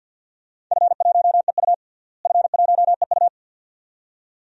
Morse code, heard as a keyed tone, about 47 wpm, 700 Hz. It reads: F1EU F1EU
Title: DXCC-100 Call Signs - 4 Characters - Repeated 2x - 50wpm